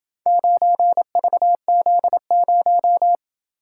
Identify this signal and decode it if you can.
Morse code, heard as a keyed tone, about 27 wpm, 700 Hz. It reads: 9VZ0